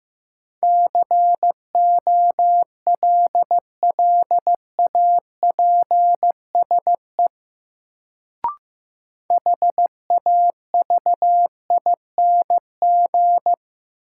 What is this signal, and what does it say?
Morse code, heard as a keyed tone, about 15 wpm, 700 Hz. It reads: COLLAPSE HAVING
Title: Sets of 5 Words Encore - New Word Drill - 15wpm